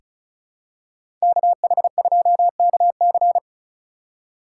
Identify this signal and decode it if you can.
Morse code, heard as a keyed tone, about 36 wpm, 700 Hz. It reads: KH2KC
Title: USA Call Signs - 5 Characters - 2x Character Spacing - 35wpm